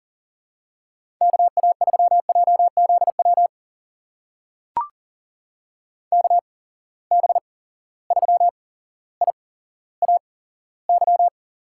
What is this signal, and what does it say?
Morse code, heard as a keyed tone, about 40 wpm, spaced as for 17 wpm, 700 Hz. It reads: KA3JZW KB3IAY
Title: USA Call Signs - 6 Characters - 8x Character Spacing - 40wpm